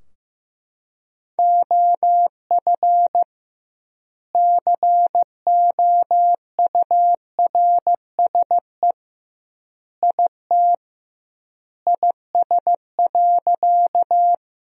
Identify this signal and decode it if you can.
Morse code, heard as a keyed tone, about 16 wpm, 700 Hz. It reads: OF COURSE IT IS.
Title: 2x Word-Spacing - Sentences from Top 500 Words - 15wpm